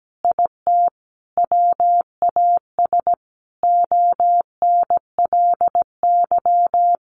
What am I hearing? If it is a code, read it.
Morse code, heard as a keyed tone, about 17 wpm, 700 Hz. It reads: IT WAS ONLY
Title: Sentences from Top 1200 Words 17wpm